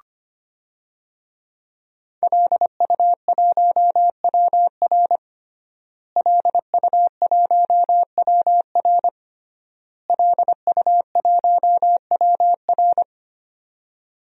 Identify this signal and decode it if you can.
Morse code, heard as a keyed tone, about 25 wpm, 700 Hz. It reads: LU1WR LU1WR LU1WR
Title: DXCC-100 Call Signs - 5 Characters - Repeated 3x - 25wpm